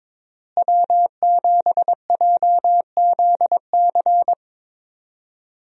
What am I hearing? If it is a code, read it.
Morse code, heard as a keyed tone, about 22 wpm, 700 Hz. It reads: W7JZC
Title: USA Call Signs - 5 Characters - Repeated 2x - 22wpm